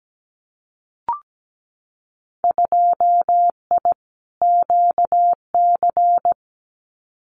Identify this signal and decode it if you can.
Morse code, heard as a keyed tone, about 17 wpm, 700 Hz. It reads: 2I QC
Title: Winter Field Day Exchanges 17wpm